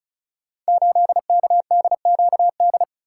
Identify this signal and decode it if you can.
Morse code, heard as a keyed tone, about 36 wpm, 700 Hz. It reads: 8KDQD
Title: Code Groups - 5 Letter:Numbers - 35wpm